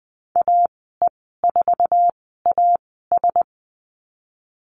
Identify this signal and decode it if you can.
Morse code, heard as a keyed tone, about 20 wpm, spaced as for 15 wpm, 700 Hz. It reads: AE4AS
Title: 055 Lesson 10g - Letter r - Call Signs 20wpm